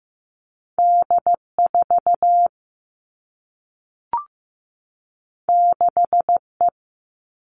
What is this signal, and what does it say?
Morse code, heard as a keyed tone, about 15 wpm, 700 Hz. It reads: D4 6E
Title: All Cut Numbers - 2-Digit 15wpm